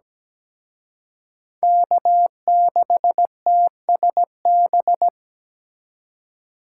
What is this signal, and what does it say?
Morse code, heard as a keyed tone, about 20 wpm, 700 Hz. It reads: K6TSB
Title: Speed Racing - USA 1x3 Call Signs 17wpm